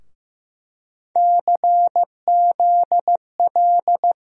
Morse code, heard as a keyed tone, about 15 wpm, 700 Hz. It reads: CZL
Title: USA Call Sign Postfixes - 3 Letters 15wpm